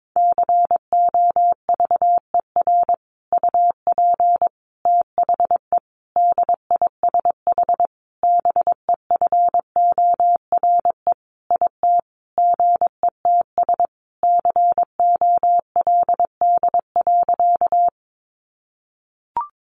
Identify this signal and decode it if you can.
Morse code, heard as a keyed tone, about 22 wpm, 700 Hz. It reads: COVER UP THE DISH BEFORE IT GETS COLD.
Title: Sentences from Top 1300 Words 22wpm